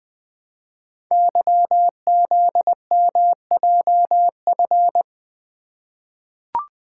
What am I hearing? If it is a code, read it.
Morse code, heard as a keyed tone, about 20 wpm, 700 Hz. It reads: YZMJF
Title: Code Groups - 5 Letters - 20wpm